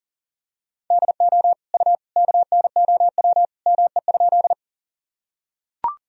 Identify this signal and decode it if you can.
Morse code, heard as a keyed tone, about 38 wpm, 700 Hz. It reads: DO U KNOW ME?
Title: Sentences from Top 200 Words and Abbreviations 40wpm